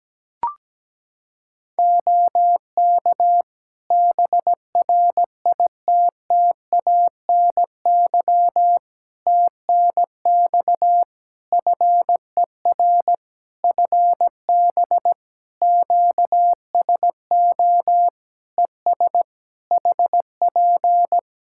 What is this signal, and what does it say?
Morse code, heard as a keyed tone, about 17 wpm, 700 Hz. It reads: OK BRITTANY TNX FER FB QSO ES HP
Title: K1USN SST QSO Practice 17wpm